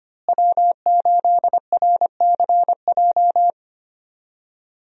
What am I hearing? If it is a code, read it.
Morse code, heard as a keyed tone, about 25 wpm, 700 Hz. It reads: W8RCJ
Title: USA Call Signs - 5 Characters - Repeated 3x - 25wpm